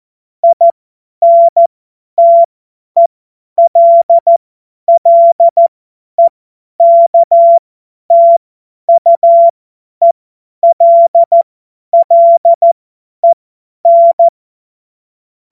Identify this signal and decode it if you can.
Morse code, heard as a keyed tone, about 14 wpm, spaced as for 10 wpm, 680 Hz. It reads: INTELLEKTUELLEN
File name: Morsekurs / Witze / Witz 04. Mai 2021, 50 F70